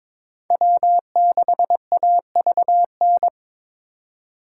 Morse code, heard as a keyed tone, about 26 wpm, 700 Hz. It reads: W6AVN
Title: Speed Racing - USA 1x3 Call Signs 22wpm